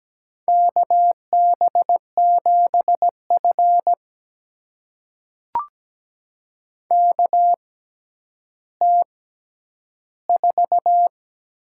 Morse code, heard as a keyed tone, about 17 wpm, spaced as for 10 wpm, 700 Hz. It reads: KB7F KT4
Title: USA Call Signs - 6x Character Spacing - 17wpm